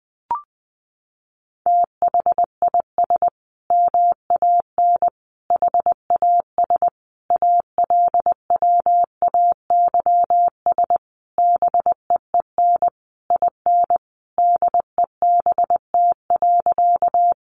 Morse code, heard as a keyed tone, about 20 wpm, 700 Hz. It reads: THIS MAN HAS ALWAYS BEEN IN DEBT.
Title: Sentences from Top 1300 Words 20wpm